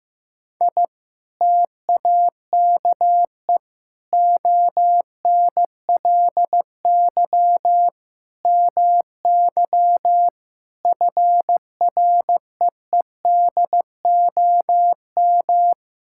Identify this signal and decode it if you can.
Morse code, heard as a keyed tone, about 15 wpm, 700 Hz. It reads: I TAKE ONLY MY FREEDOM
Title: Sentences from Top 1000 Words - Review Part 5 - 15wpm